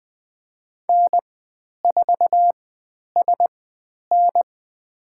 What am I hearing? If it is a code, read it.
Morse code, heard as a keyed tone, about 20 wpm, spaced as for 10 wpm, 700 Hz. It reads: N4SN
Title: 109 Lesson 18g - Letter m - Call Signs 20wpm